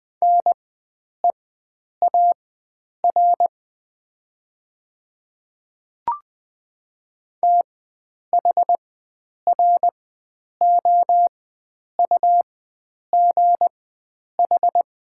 Morse code, heard as a keyed tone, about 20 wpm, spaced as for 9 wpm, 690 Hz. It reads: NEAR THROUGH
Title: 4x Character Spacing - Top 300 Words - 20wpm